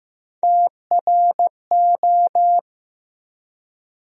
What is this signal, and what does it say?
Morse code, heard as a keyed tone, about 15 wpm, 700 Hz. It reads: TRO
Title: USA Call Sign Postfixes - 3 Letters 15wpm